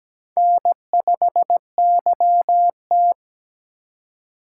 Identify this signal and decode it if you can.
Morse code, heard as a keyed tone, about 17 wpm, 700 Hz. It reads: N5YT
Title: USA Call Signs - 4 Characters - 4x Character Spacing - 17wpm